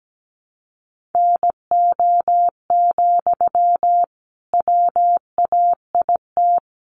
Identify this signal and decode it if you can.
Morse code, heard as a keyed tone, about 17 wpm, 700 Hz. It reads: NO, WAIT
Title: Sentences from Top 1500 Words 17wpm